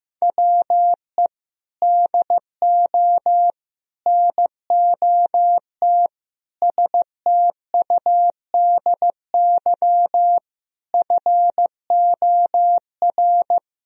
Morse code, heard as a keyed tone, about 15 wpm, 700 Hz. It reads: WE DO NOT STUDY FOR
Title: Sentences from Top 1000 Words - Review Part 1 - 15wpm